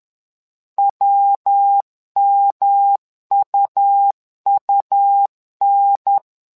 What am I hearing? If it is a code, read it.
Morse code, heard as a keyed tone, about 10 wpm, 800 Hz. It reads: WMUUN
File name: LEZIONI / 3 MWRO / 3 MWRO 10wpm 2148